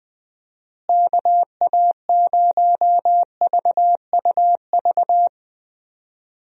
Morse code, heard as a keyed tone, about 20 wpm, 700 Hz. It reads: KA0VUV
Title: USA Call Signs - 6 Characters - Repeated 2x - 20wpm